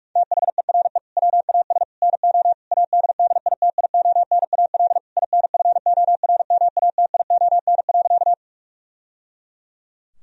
Morse code, heard as a keyed tone, about 41 wpm, 700 Hz. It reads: THERE WAS NO ADDITIONAL INFORMATION.
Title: Sentences from Top 1000 Words - Review Part 4 - 45wpm